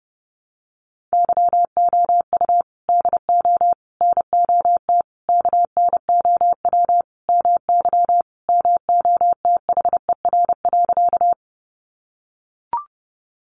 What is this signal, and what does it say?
Morse code, heard as a keyed tone, about 30 wpm, 700 Hz. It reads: YOU DO NOT KNOW MY MOTHER.
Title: Sentences from Top 600 Words 30wpm